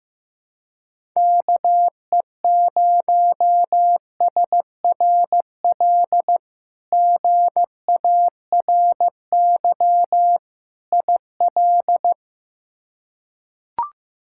Morse code, heard as a keyed tone, about 15 wpm, 700 Hz. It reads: KE0SRL GARY IL
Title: K1USN SST QSO Practice - 15wpm